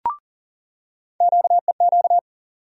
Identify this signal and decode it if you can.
Morse code, heard as a keyed tone, about 40 wpm, 700 Hz. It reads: QEQ
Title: USA Call Sign Postfixes - 3 Letters 40wpm